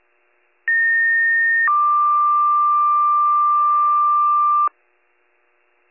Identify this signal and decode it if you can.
Two-tone page, heard as a one-second tone, then a three-second tone, about 1850 then 1180 Hz. It is heard from 0.7 to 4.7 s.